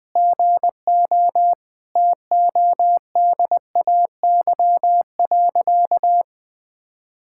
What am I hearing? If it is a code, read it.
Morse code, heard as a keyed tone, about 20 wpm, 700 Hz. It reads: GO TODAY.
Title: Sentences with 5-Word Phrases - 20wpm